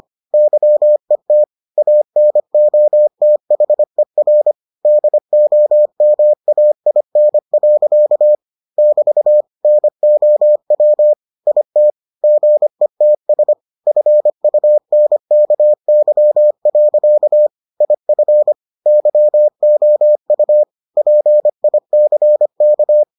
Morse code, heard as a keyed tone, about 25 wpm, 600 Hz. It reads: YET ANOTHER DOMAIN. <BT> NOW IT GETS FUNKY. IF YOU PICK